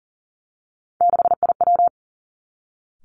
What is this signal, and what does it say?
Morse code, heard as a keyed tone, about 38 wpm, 700 Hz. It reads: 6IW
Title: Three Challenging Letter-Number Combinations 40wpm